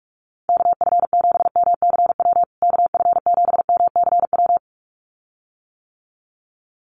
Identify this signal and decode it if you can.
Morse code, heard as a keyed tone, about 44 wpm, 700 Hz. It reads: KF7MCW KF7MCW
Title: USA Call Signs Repeated 2x 45wpm